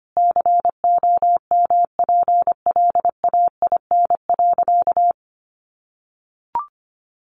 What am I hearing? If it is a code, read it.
Morse code, heard as a keyed tone, about 25 wpm, 700 Hz. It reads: COMPLAIN.
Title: Sentences from Top 1300 Words 25wpm